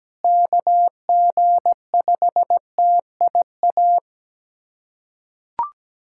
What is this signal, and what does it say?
Morse code, heard as a keyed tone, about 17 wpm, 700 Hz. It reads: KG5TIA
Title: USA Callsign DE Callsign 17wpm